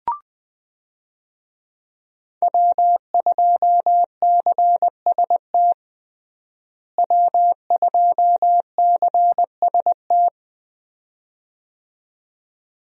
Morse code, heard as a keyed tone, about 21 wpm, 700 Hz. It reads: W2CST W2CST
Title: USA Call Signs - 5 Characters - Repeated 2x - 20wpm